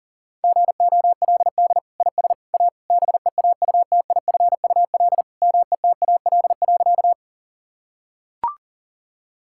Morse code, heard as a keyed tone, about 40 wpm, 700 Hz. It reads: GOLD IS A BEAUTIFUL METAL.
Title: Sentences from Top 1000 Words - Review Part 6 - 40wpm